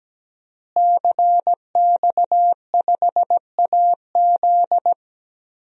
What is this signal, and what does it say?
Morse code, heard as a keyed tone, about 17 wpm, 700 Hz. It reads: CX5AZ
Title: DXCC-100 Call Signs - 5 Characters - Repeated 3x - 17wpm